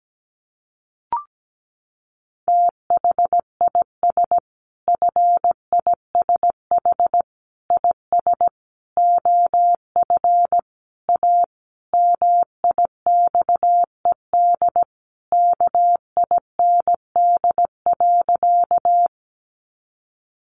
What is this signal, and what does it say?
Morse code, heard as a keyed tone, about 17 wpm, 700 Hz. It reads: THIS FISH IS OF A MIXED KIND.
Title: Sentences from Top 1500 Words 17wpm